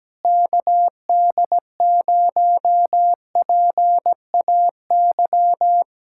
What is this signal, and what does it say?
Morse code, heard as a keyed tone, about 17 wpm, 700 Hz. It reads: KD0PAY
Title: Speed-Racing Common QSO Phrases - 17wpm